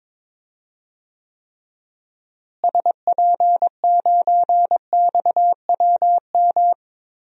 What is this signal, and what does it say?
Morse code, heard as a keyed tone, about 22 wpm, 700 Hz. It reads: SP9XWM
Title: DXCC-100 Call Signs - 6 Characters - Repeated 3x - 22wpm